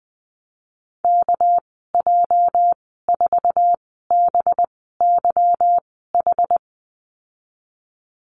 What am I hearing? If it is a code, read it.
Morse code, heard as a keyed tone, about 20 wpm, spaced as for 14 wpm, 700 Hz. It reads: KJ4BYH
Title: 238 Lesson 38g - Letter q - Call Signs 20wpm